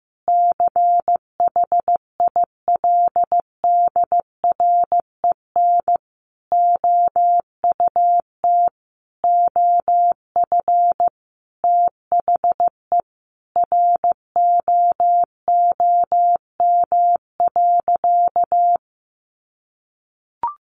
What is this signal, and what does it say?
Morse code, heard as a keyed tone, about 15 wpm, 700 Hz. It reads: CHILDREN OUT OF THE ROOM.
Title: Sentences from Top 1500 Words 15wpm